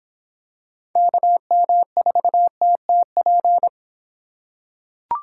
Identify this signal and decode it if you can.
Morse code, heard as a keyed tone, about 23 wpm, 700 Hz. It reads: KM4TTP